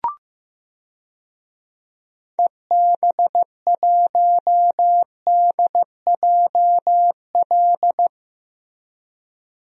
Morse code, heard as a keyed tone, about 15 wpm, 700 Hz. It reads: EB1DJL
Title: DXCC-100 Call Signs - 6 Characters - Repeated 2x - 15wpm